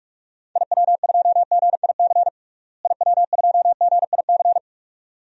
Morse code, heard as a keyed tone, about 46 wpm, 700 Hz. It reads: IW2GIC IW2GIC
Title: DXCC-100 Call Signs - 6 Characters - Repeated 2x - 45wpm